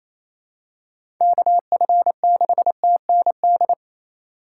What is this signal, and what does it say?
Morse code, heard as a keyed tone, about 27 wpm, 700 Hz. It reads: KF6TND
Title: USA Call Signs - 6 Characters - 8x Character Spacing - 28wpm